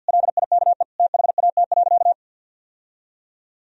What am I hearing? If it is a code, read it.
Morse code, heard as a keyed tone, about 52 wpm, 700 Hz. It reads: LIKE THAT.
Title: Sentences from Top 1000 Words - Review Part 3 - 50wpm